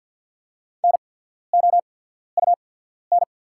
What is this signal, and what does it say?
Morse code, heard as a keyed tone, about 48 wpm, spaced as for 14 wpm, 690 Hz. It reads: NOUN
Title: 8x Character Spacing - Top 500 Words - 50wpm